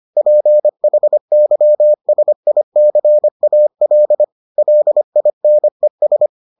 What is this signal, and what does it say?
Morse code, heard as a keyed tone, about 25 wpm, 600 Hz. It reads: PHYSICAL LINES